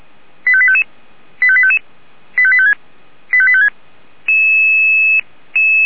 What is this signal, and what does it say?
Five-tone selective call (ZVEI), heard as quick runs of five tones, tones of about 70 ms. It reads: 85855 85855 85856 85856